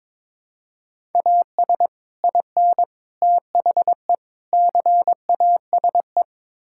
Morse code, heard as a keyed tone, about 22 wpm, 700 Hz. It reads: AS IN THE CASE
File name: practice / 22wpm / Build A Phrase - 5-Word Phrases - 22wpm